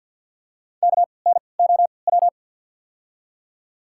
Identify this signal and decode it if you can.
Morse code, heard as a keyed tone, about 54 wpm, 690 Hz. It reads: K N O W
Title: 3x Character Spacing - Top 200 Words - 50wpm